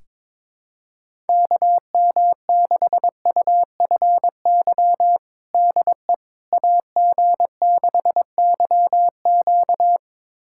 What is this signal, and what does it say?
Morse code, heard as a keyed tone, about 22 wpm, 700 Hz. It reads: KM6UFY DE AG6YQ
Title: USA Callsign DE Callsign 22wpm